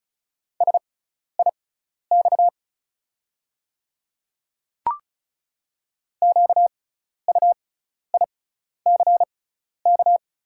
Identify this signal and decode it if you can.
Morse code, heard as a keyed tone, about 36 wpm, spaced as for 12 wpm, 700 Hz. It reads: SIX QUICK
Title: 6x Character Spacing - Top 500 Words - 35wpm